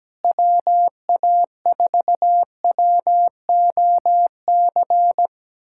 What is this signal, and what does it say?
Morse code, heard as a keyed tone, about 17 wpm, 700 Hz. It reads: WA4WOC
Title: USA Callsign DE Callsign - 6 Characters - 17wpm